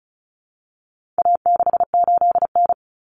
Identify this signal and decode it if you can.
Morse code, heard as a keyed tone, about 34 wpm, 700 Hz. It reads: A68N
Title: Common Cut Numbers - 4-Digit 35wpm